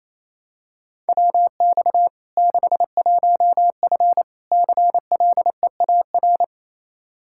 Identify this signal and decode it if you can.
Morse code, heard as a keyed tone, about 28 wpm, 700 Hz. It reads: WX 61F CLEAR
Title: K1USN SST QSO Practice 28wpm